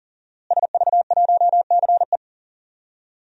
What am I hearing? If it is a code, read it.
Morse code, heard as a keyed tone, about 40 wpm, 700 Hz. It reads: SV1CE
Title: DXCC-100 Call Signs - 5 Characters - Repeated 2x - 40wpm